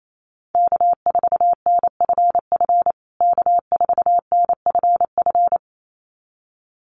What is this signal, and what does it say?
Morse code, heard as a keyed tone, about 28 wpm, 700 Hz. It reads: K4NFF K4NFF
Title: USA Call Signs Repeated 2x 28wpm